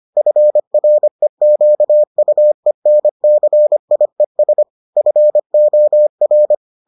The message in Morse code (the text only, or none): FREQUENCIES FOR